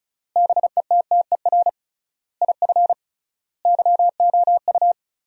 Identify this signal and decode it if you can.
Morse code, heard as a keyed tone, about 36 wpm, 700 Hz. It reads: BETTER IF YOU
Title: 3x Word-Spacing - Sentences from Top 400 Words - 35wpm